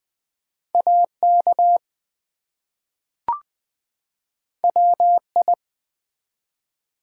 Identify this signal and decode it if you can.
Morse code, heard as a keyed tone, about 20 wpm, 700 Hz. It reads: AK WI
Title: ARRL Sections v2 - 20wpm